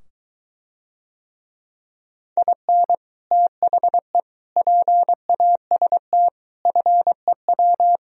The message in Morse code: IN THE PAST FEW

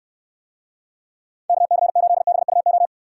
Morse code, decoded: KC8BUO

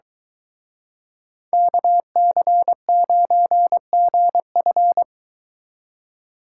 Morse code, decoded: KC9GF

